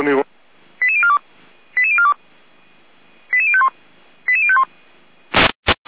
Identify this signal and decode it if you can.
Five-tone selective call (ZVEI), heard as quick runs of five tones, tones of about 70 ms, with speech. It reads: 88042 88042 88051 88051